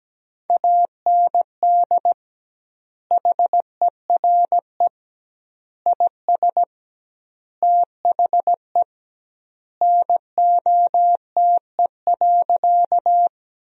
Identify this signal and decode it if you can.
Morse code, heard as a keyed tone, about 17 wpm, 700 Hz. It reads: AND HERE IS THE NOTE.
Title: 2x Word-Spacing - Sentences from Top 500 Words - 17wpm